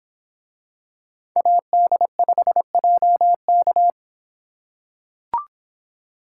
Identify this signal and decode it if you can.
Morse code, heard as a keyed tone, about 24 wpm, 700 Hz. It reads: AD5JK